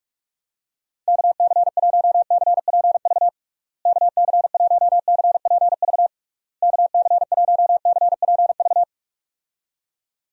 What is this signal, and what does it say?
Morse code, heard as a keyed tone, about 46 wpm, 700 Hz. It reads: KC1CPV KC1CPV KC1CPV
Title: USA Call Signs - 6 Characters - Repeated 3x - 45wpm